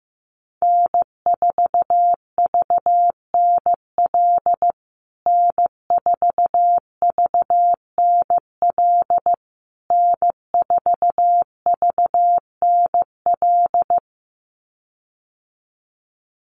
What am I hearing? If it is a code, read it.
Morse code, heard as a keyed tone, about 15 wpm, 700 Hz. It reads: N4VNL N4VNL N4VNL
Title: USA Call Signs Repeated 3x 15wpm